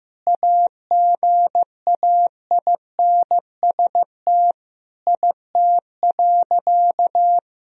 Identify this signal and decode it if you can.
Morse code, heard as a keyed tone, about 15 wpm, 700 Hz. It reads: AGAINST IT.
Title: Sentences from Top 1000 Words - Review Part 1 - 15wpm